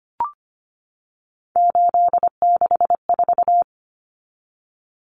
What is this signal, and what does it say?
Morse code, heard as a keyed tone, about 25 wpm, 700 Hz. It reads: 864
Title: Common Cut Numbers - 3-Digit 25wpm